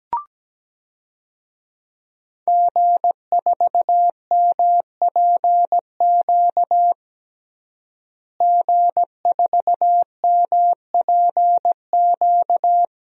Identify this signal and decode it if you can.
Morse code, heard as a keyed tone, about 17 wpm, 700 Hz. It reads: G4MPQ G4MPQ
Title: DXCC-100 Call Signs - 5 Characters - Repeated 3x - 17wpm